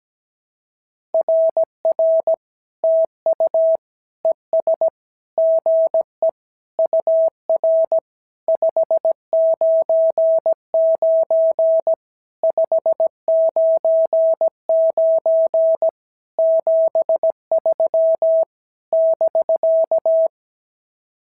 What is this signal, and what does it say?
Morse code, heard as a keyed tone, about 17 wpm, 650 Hz. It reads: RR TU ES GE UR 599 599 73 <BK>
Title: SOTA QSO Practice 17wpm